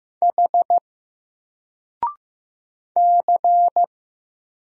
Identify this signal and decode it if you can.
Morse code, heard as a keyed tone, about 15 wpm, 700 Hz. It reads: H C
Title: Single Letter-Number 15wpm